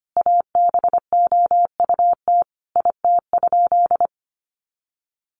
Morse code, heard as a keyed tone, about 25 wpm, 700 Hz. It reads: ABOUT IT?
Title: Sentences from Top 100 Words 25wpm